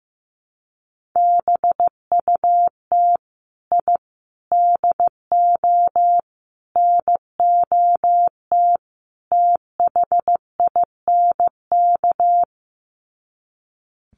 Morse code, heard as a keyed tone, about 14 wpm, 700 Hz. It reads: BUT I DO NOT THINK
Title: Sets of 5 Words 15wpm